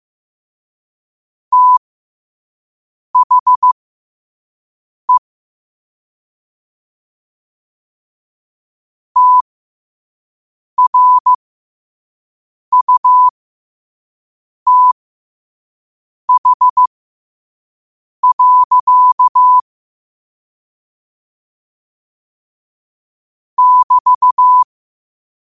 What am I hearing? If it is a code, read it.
Morse code, heard as a keyed tone, about 15 wpm, spaced as for 5 wpm, 1000 Hz. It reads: THE TRUTH. <BT>
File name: qotdfiles / 5wpm